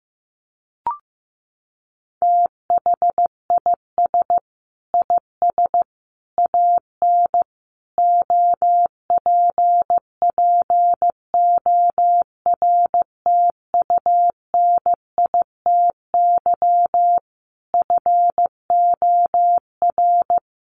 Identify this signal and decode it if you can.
Morse code, heard as a keyed tone, about 15 wpm, 700 Hz. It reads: THIS IS AN OPPORTUNITY FOR
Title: Sentences from Top 1000 Words 15wpm